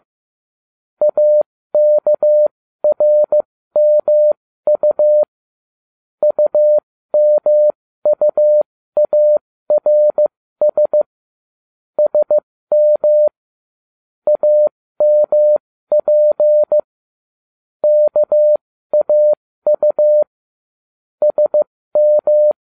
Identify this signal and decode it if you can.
Morse code, heard as a keyed tone, about 15 wpm, spaced as for 13 wpm, 600 Hz. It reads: AKRMU UMUARS SM AMP KAU SM